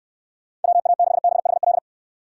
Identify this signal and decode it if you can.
Morse code, heard as a keyed tone, about 69 wpm, 700 Hz. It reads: WA6BHL